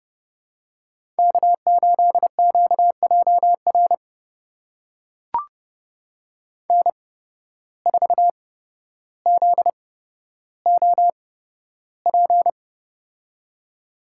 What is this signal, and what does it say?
Morse code, heard as a keyed tone, about 30 wpm, spaced as for 13 wpm, 700 Hz. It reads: K8QJR N4ZOP
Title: USA Call Signs - 8x Character Spacing - 30wpm